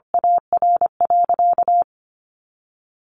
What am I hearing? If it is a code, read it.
Morse code, heard as a keyed tone, about 25 wpm, 700 Hz. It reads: AR.
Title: Sentences from Top 200 Words 25wpm